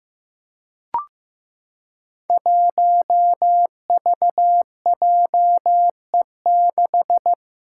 Morse code, heard as a keyed tone, about 15 wpm, 700 Hz. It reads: 1VJE6
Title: Five Challenging Letter-Number Combinations 15wpm